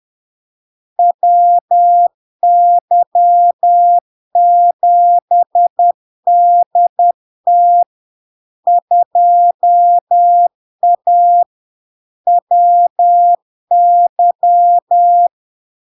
Morse code, heard as a keyed tone, about 10 wpm, 700 Hz. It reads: WY7DT 2A WY
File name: LARK-FD-2022-Log-in-CW-10wpm-700-Hz